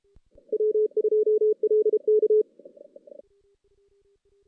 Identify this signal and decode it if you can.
Morse code, heard as a keyed tone, about 29 wpm, 440 Hz. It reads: W2LK
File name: 2016 JAN CW NAQP / 2016 Jan naqp k5ot - 0-5